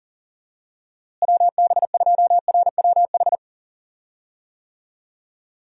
Morse code, heard as a keyed tone, about 37 wpm, 700 Hz. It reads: WB2RWH
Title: USA Call Signs - 6 Characters - Repeated 3x - 40wpm